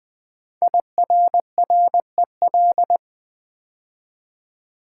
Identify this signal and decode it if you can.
Morse code, heard as a keyed tone, about 20 wpm, 700 Hz. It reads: IRREL
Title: Build A Word - 5 Letters At A Time - 10-Letter Words - 20wpm